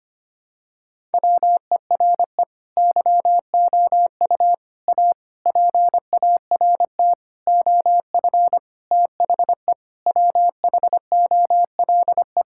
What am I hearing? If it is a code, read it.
Morse code, heard as a keyed tone, about 25 wpm, 700 Hz. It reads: WERE YOU A PART OF THE WHOLE
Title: Sentences from Top 1000 Words - Review Part 6 - 25wpm